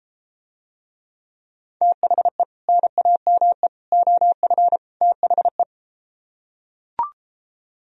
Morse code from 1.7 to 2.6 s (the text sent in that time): THE